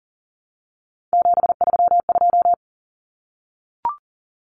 Morse code, heard as a keyed tone, about 37 wpm, 700 Hz. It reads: 732
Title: Numbers - 3-Digit 40wpm